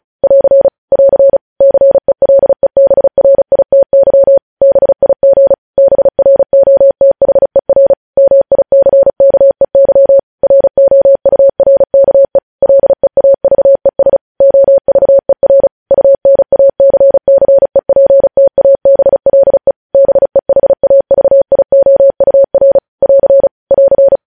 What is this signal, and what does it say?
Morse code, heard as a keyed tone, about 36 wpm, 550 Hz. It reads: <AR> <AR> CELEBRITY BIG BROTHER MICKEY ROURKE LEAVES OVER UNACCEPTABLE BEHAVIOUR <AR> <AR>
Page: News Headlines in Morse Code at 35 WPM